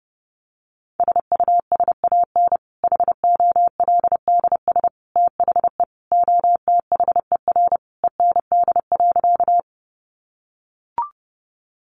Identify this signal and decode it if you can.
Morse code, heard as a keyed tone, about 28 wpm, 700 Hz. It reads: SUSAN HOLDS THE OTHER END.
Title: Easy Phrases 30wpm